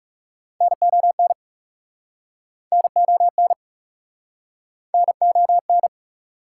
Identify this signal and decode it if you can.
Morse code, heard as a keyed tone, about 41 wpm, 690 Hz. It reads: NON NON NON